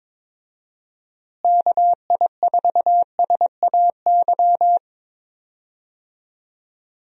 Morse code, heard as a keyed tone, about 22 wpm, 700 Hz. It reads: KI4SAY
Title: USA Call Signs - 6 Characters - Repeated 2x - 22wpm